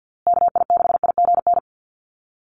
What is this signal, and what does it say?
Morse code, heard as a keyed tone, about 50 wpm, 700 Hz. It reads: KI6IGN
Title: USA Call Signs Repeated 2x 50wpm